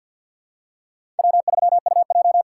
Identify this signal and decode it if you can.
Morse code, heard as a keyed tone, about 53 wpm, 700 Hz. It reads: W3UJ
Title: USA Call Signs - 4 Characters - 3x Character Spacing - 50wpm